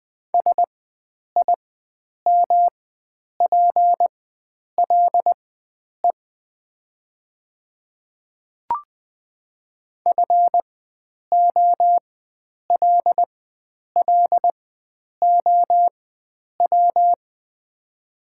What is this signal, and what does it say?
Morse code, heard as a keyed tone, about 20 wpm, spaced as for 9 wpm, 700 Hz. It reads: SIMPLE FOLLOW
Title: 4x Character Spacing - Top 500 Words - 20wpm